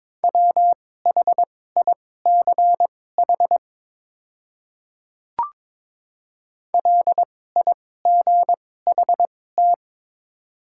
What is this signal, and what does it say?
Morse code, heard as a keyed tone, about 22 wpm, spaced as for 16 wpm, 700 Hz. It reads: WHICH LIGHT
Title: 2x Character Spacing - Top 200 Words - 22wpm